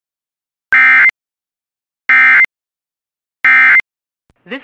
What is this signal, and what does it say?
Emergency Alert System (SAME) tones, heard as harsh buzzing bursts, with speech. It is heard from 0.7 to 3.8 s.
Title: Relaying Severe Thunderstorm Warning from KOD/DAS.